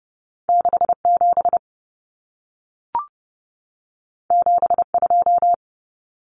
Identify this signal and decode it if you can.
Morse code, heard as a keyed tone, about 30 wpm, 700 Hz. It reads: 67 72
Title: Common Cut Numbers - 2-Digit 30wpm